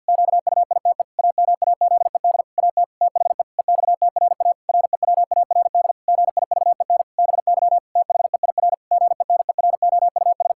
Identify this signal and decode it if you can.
Morse code, heard as a keyed tone, about 50 wpm, 700 Hz. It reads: QUITE AMAZED AT THE EXTRA REWARD GIVEN BY THEIR GENEROUS
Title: Pangram Challenge v2 - 50wpm